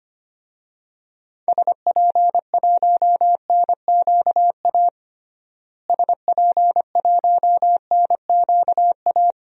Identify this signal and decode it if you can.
Morse code, heard as a keyed tone, about 25 wpm, 700 Hz. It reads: SP1NQA SP1NQA
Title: DXCC-100 Call Signs - 6 Characters - Repeated 3x - 25wpm